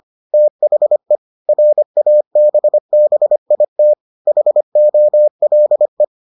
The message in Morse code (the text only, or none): THE RABBIT HOLE